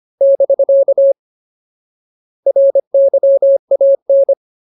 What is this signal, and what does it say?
Morse code, heard as a keyed tone, about 25 wpm, 550 Hz. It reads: <BK> RYAN
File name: Ultimate Word List / 25WPM Ultimate